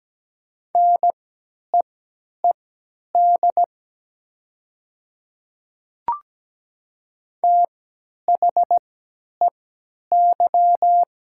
Morse code, heard as a keyed tone, about 17 wpm, spaced as for 10 wpm, 700 Hz. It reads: NEED THEY
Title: 3x Character Spacing - Top 300 Words - 17wpm